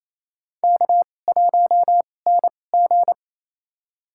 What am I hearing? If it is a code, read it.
Morse code, heard as a keyed tone, about 28 wpm, spaced as for 20 wpm, 700 Hz. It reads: K1NG
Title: USA Call Signs - 4 Characters - 2x Character Spacing - 28wpm